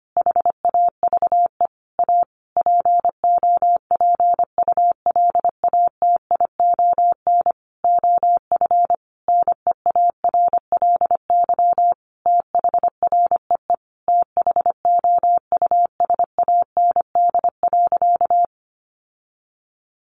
Morse code, heard as a keyed tone, about 25 wpm, 700 Hz. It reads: HAVE A POPULATION OF NEARLY THREE THOUSAND.
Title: Sentences from Top 1300 Words 25wpm